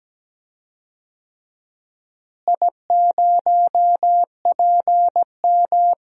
Morse code, heard as a keyed tone, about 17 wpm, 700 Hz. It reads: I0PM
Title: DXCC-100 Call Signs - 4 Characters - Repeated 3x - 17wpm